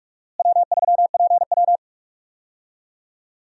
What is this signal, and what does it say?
Morse code, heard as a keyed tone, about 46 wpm, 700 Hz. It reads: W3PW
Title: USA Call Signs - 4 Characters - Repeated 3x - 45wpm